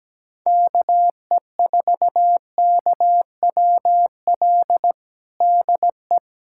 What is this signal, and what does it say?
Morse code, heard as a keyed tone, about 17 wpm, 700 Hz. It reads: KE4KWL DE
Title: USA Callsign DE Callsign - 6 Characters - 17wpm